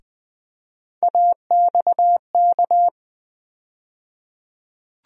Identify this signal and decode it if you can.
Morse code, heard as a keyed tone, about 20 wpm, 700 Hz. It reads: AXK
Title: USA Call Sign Postfixes - 3 Letters 20wpm